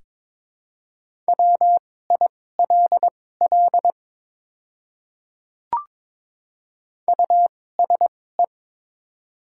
Morse code, heard as a keyed tone, about 22 wpm, spaced as for 16 wpm, 700 Hz. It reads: WILL USE
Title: 2x Character Spacing - Top 100 Words - 22wpm